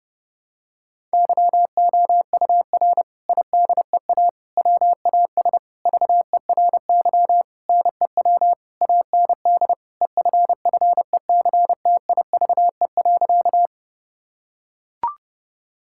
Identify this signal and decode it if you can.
Morse code, heard as a keyed tone, about 30 wpm, 700 Hz. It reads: YOUR IDEA WAS VERY NEW AND EFFECTIVE.
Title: Sentences from Top 1000 Words - Review Part 6 - 30wpm